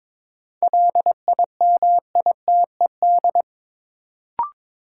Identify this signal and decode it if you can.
Morse code, heard as a keyed tone, about 22 wpm, 700 Hz. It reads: LIMITED
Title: Sentences from Top 900 Words - New Word Drill - 22wpm